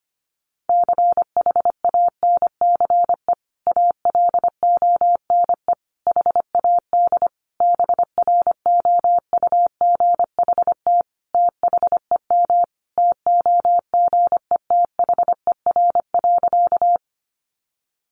Morse code, heard as a keyed tone, about 25 wpm, 700 Hz. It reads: CHANCE ALONE HAD BROUGHT THEM TOGETHER.
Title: Sentences from Top 1000 Words 25wpm